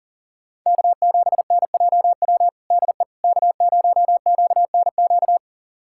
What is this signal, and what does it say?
Morse code, heard as a keyed tone, about 40 wpm, 700 Hz. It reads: K7NJW DE K0QNQ
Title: USA Callsign DE Callsign - 5 Characters - 40wpm